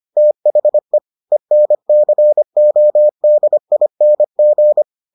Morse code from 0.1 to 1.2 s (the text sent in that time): THE